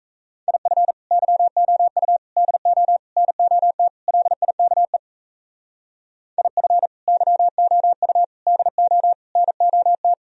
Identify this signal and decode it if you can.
Morse code, heard as a keyed tone, about 37 wpm, 700 Hz. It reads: IF YOU DO NOT LIKE IF YOU DO NOT